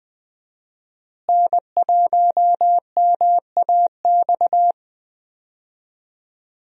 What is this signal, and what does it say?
Morse code, heard as a keyed tone, about 20 wpm, 700 Hz. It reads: N1MAX
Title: USA Call Signs - 5 Characters - Repeated 3x - 20wpm